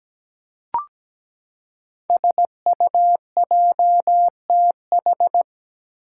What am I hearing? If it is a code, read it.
Morse code, heard as a keyed tone, about 17 wpm, 700 Hz. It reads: SUJTH
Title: Code Groups - 5 Letter:Numbers - 17wpm